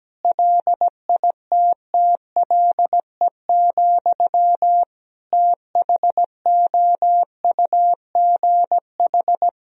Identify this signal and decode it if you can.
Morse code, heard as a keyed tone, about 17 wpm, 700 Hz. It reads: LITTLE, THOUGH
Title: Sentences from Top 400 Words and Abbreviations 17wpm